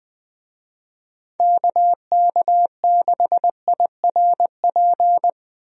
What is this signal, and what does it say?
Morse code, heard as a keyed tone, about 20 wpm, 700 Hz. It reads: KK6IRP
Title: USA Call Signs - 6 Characters - 20wpm